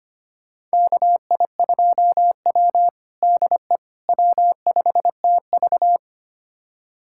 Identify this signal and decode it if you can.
Morse code, heard as a keyed tone, about 25 wpm, 700 Hz. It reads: KI2W DE W5TV
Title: USA Callsign DE Callsign - 4 Characters - 25wpm